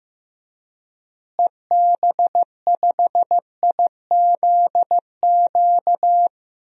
Morse code, heard as a keyed tone, about 15 wpm, 700 Hz. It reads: EB5IZQ
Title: DXCC-100 Call Signs - 6 Characters - Repeated 3x - 15wpm